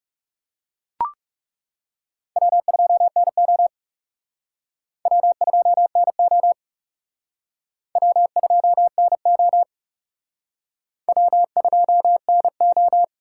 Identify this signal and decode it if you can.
Morse code, heard as a keyed tone, about 37 wpm, 700 Hz. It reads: W2NO W2NO W2NO W2NO
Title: Speed Racing - USA Call Signs 30wpm